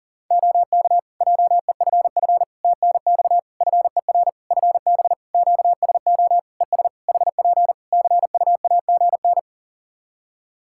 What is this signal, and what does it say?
Morse code, heard as a keyed tone, about 40 wpm, 700 Hz. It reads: OK JEFF TNX FER FB QSO ES HP CUAGN
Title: Practice QSO - Exchange 4 - Ending - 40wpm